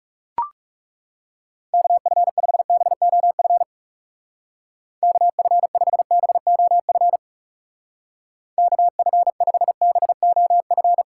Morse code, heard as a keyed tone, about 40 wpm, 700 Hz. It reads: KF5BOF KF5BOF KF5BOF